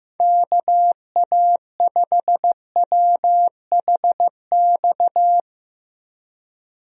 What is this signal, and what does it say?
Morse code, heard as a keyed tone, about 15 wpm, 700 Hz. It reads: KA5WHX
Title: USA Call Signs - 6 Characters - Repeated 3x - 15wpm